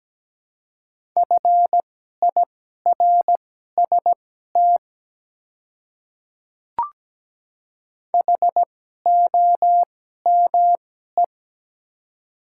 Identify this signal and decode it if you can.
Morse code, heard as a keyed tone, about 17 wpm, spaced as for 12 wpm, 700 Hz. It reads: FIRST HOME
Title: 2x Character Spacing - Top 200 Words - 17wpm